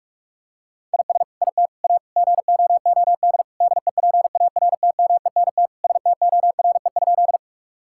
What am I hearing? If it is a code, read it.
Morse code, heard as a keyed tone, about 46 wpm, 700 Hz. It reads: IS IT A GOOD DEPARTMENT STORE?
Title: Sentences from Top 1000 Words - Review Part 5 - 45wpm